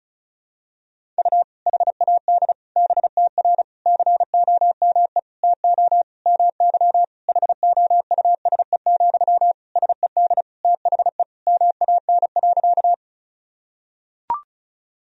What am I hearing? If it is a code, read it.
Morse code, heard as a keyed tone, about 36 wpm, 700 Hz. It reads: U HAD BTR COME TO MY HOUSE, SED THE MAN.
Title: Sentences from Top 400 Words and Abbreviations 35wpm